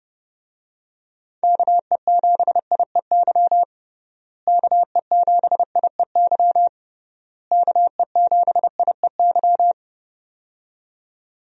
Morse code, heard as a keyed tone, about 30 wpm, 700 Hz. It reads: KE7IEY KE7IEY KE7IEY
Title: USA Call Signs - 6 Characters - Repeated 3x - 30wpm